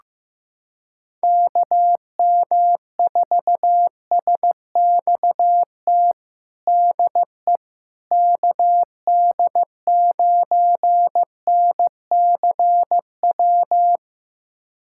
Morse code, heard as a keyed tone, about 16 wpm, 700 Hz. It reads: KM4SXT DE KD9NCW
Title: USA Callsign DE Callsign - 6 Characters - 15wpm